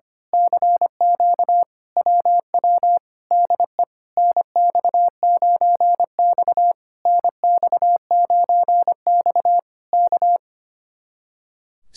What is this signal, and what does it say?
Morse code, heard as a keyed tone, about 25 wpm, 700 Hz. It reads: CQ WW DE NX9X NX9X K
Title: Speed-Racing Common QSO Phrases - 25wpm